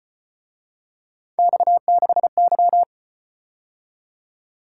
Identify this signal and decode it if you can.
Morse code, heard as a keyed tone, about 34 wpm, 700 Hz. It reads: X6Y